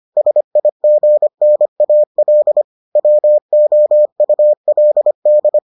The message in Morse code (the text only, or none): SIGNAL WOULD